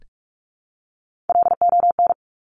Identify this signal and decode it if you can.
Morse code, heard as a keyed tone, about 46 wpm, 700 Hz. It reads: LON
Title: Top Three Letter Combinations in Words 45wpm